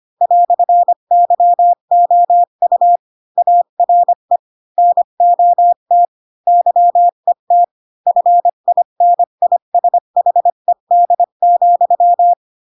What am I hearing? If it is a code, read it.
Morse code, heard as a keyed tone, about 25 wpm, 700 Hz. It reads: "YOU ARE NOT YET FINISHED,
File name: Pinocchio / Pinocchio ch3 20 25 ik1hge